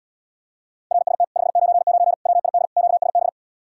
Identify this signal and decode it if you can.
Morse code, heard as a keyed tone, about 80 wpm, 700 Hz. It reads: RST 599 WID QSB